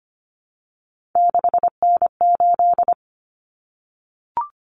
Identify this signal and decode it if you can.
Morse code, heard as a keyed tone, about 25 wpm, 700 Hz. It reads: 6N8
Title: Common Cut Numbers - 3-Digit 25wpm